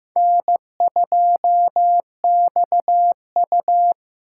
Morse code, heard as a keyed tone, about 18 wpm, 700 Hz. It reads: N2XU
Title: Speed Racing - USA 1x2 Call Signs 15wpm